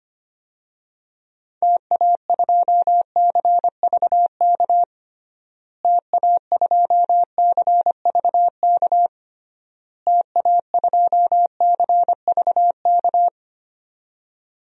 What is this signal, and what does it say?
Morse code, heard as a keyed tone, about 25 wpm, 700 Hz. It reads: TA2CVK TA2CVK TA2CVK
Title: DXCC-100 Call Signs - 6 Characters - Repeated 3x - 25wpm